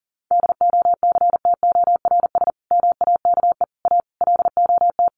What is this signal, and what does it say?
Morse code, heard as a keyed tone, about 40 wpm, 700 Hz. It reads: DOCTORS MAKE A LOT
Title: Easy Phrases 40wpm